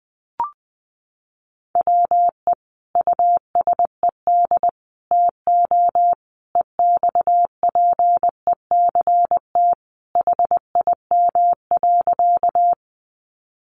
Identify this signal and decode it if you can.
Morse code, heard as a keyed tone, about 20 wpm, 700 Hz. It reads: WE USED TO EXPECT HIM.
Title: Sentences from Top 1100 Words 20wpm